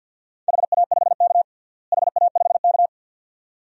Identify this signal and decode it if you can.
Morse code, heard as a keyed tone, about 47 wpm, 700 Hz. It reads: HA5K HA5K
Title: DXCC-100 Call Signs - 4 Characters - Repeated 2x - 50wpm